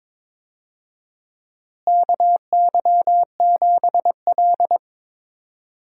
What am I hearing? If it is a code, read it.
Morse code, heard as a keyed tone, about 21 wpm, 700 Hz. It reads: KY7L
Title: USA Call Signs - 4 Characters - Repeated 2x - 22wpm